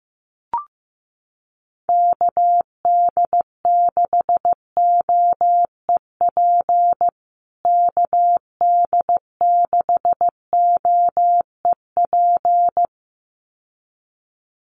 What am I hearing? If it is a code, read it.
Morse code, heard as a keyed tone, about 15 wpm, 700 Hz. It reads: KD6OEP KD6OEP
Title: USA Call Signs Repeated 2x 15wpm